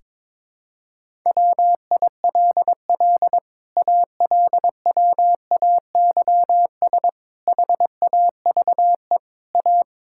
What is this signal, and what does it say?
Morse code, heard as a keyed tone, about 22 wpm, 700 Hz. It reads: WILL ALWAYS HAVE A
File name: Sets of 9 Words - 22wpm